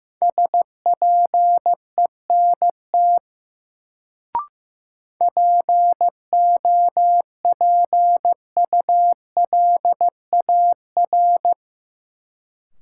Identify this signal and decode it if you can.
Morse code, heard as a keyed tone, about 15 wpm, 700 Hz. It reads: SPENT POPULAR
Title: Sets of 5 Words - New Word Drill - 15wpm